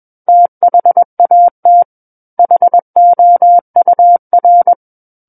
Morse code, heard as a keyed tone, about 21 wpm, 700 Hz. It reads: THAT HOUR
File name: Book One / 12-Chapter 12 - 21 WPM